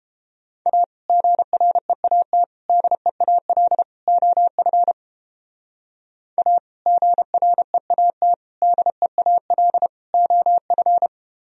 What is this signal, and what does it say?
Morse code, heard as a keyed tone, about 29 wpm, 700 Hz. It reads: A GREAT DEAL OF A GREAT DEAL OF